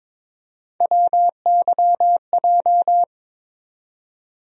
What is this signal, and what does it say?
Morse code, heard as a keyed tone, about 22 wpm, 700 Hz. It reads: WYJ
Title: USA Call Sign Postfixes - 3 Letters 22wpm